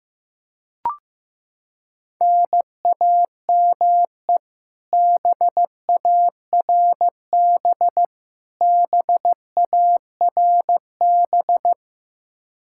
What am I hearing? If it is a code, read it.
Morse code, heard as a keyed tone, about 16 wpm, 700 Hz. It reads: NAME BARB BARB
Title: Practice QSO - Exchange 1 - Beginning - 15wpm